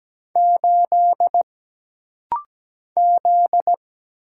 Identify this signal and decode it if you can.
Morse code, heard as a keyed tone, about 17 wpm, 700 Hz. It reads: 8 Z
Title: Single Letter-Number Alt - Rapid-Fire 17wpm